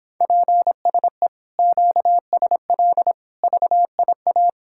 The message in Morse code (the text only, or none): PSE QSL VIA